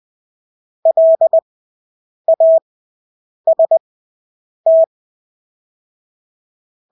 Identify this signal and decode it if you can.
Morse code, heard as a keyed tone, about 20 wpm, spaced as for 8 wpm, 650 Hz. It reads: LAST
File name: Engelske ord 20/8 wpm